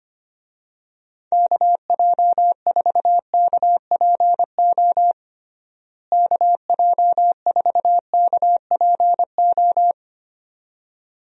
Morse code, heard as a keyed tone, about 25 wpm, 700 Hz. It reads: KJ4KPO KJ4KPO
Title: USA Call Signs - 6 Characters - Repeated 2x - 25wpm